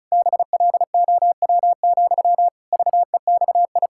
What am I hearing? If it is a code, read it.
Morse code, heard as a keyed tone, about 36 wpm, 700 Hz. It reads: BLOW, VEXI
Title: Pangram Challenge v2 - 35wpm